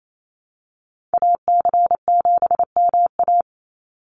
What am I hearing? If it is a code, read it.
Morse code, heard as a keyed tone, about 28 wpm, 700 Hz. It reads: AC7MA
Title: NA SOTA Activator Callsigns 28wpm